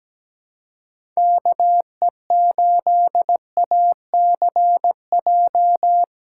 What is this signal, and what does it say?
Morse code, heard as a keyed tone, about 17 wpm, 700 Hz. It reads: KE8ACJ
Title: USA Call Signs - 6 Characters - 2x Character Spacing - 17wpm